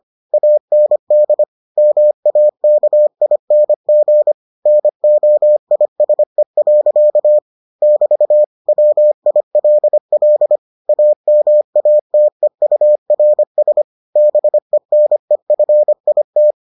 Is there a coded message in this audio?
Yes: Morse code from 0.3 to 16.5 s, heard as a keyed tone, about 25 wpm, 600 Hz. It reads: AND MAKING NOISE. <BT> WILL AMATEURS BENEFIT